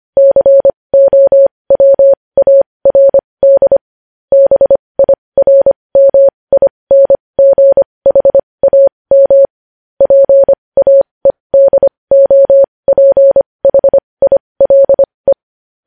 Morse code, heard as a keyed tone, about 25 wpm, spaced as for 20 wpm, 550 Hz. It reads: COWARD BIRMINGHAM PAEDOPHILE
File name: podcast / News headlines-1x-25x20